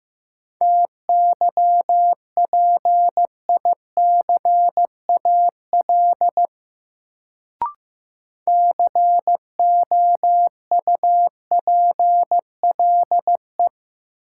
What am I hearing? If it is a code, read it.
Morse code, heard as a keyed tone, about 15 wpm, 700 Hz. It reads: TYPICAL COUPLE
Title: Sentences from Top 1100 Words - New Word Drill - 15wpm